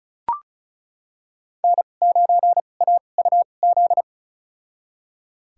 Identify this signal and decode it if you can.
Morse code, heard as a keyed tone, about 34 wpm, spaced as for 25 wpm, 700 Hz. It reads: N9AUZ
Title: USA Call Signs - 5 Characters - 2x Character Spacing - 35wpm